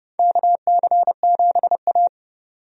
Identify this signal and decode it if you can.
Morse code, heard as a keyed tone, about 30 wpm, 700 Hz. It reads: KC7A
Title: USA Call Signs - 4 Characters - 30wpm